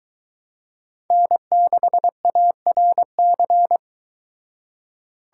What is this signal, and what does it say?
Morse code, heard as a keyed tone, about 23 wpm, 700 Hz. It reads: N6ARC